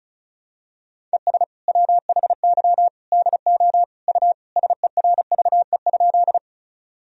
Morse code, heard as a keyed tone, about 36 wpm, 700 Hz. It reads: ES WHY DO U SERVE?
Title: Sentences from Top 400 Words and Abbreviations 35wpm